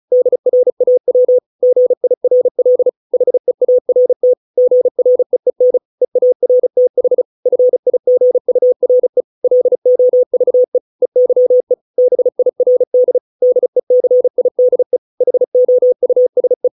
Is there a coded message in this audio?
Yes: Morse code from 0.1 to 16.7 s, heard as a keyed tone, about 34 wpm, 500 Hz. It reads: DRAW GIRL HEART GREEN EARTH FIGURE LOVE EYE BIRD DECIDE HOUSE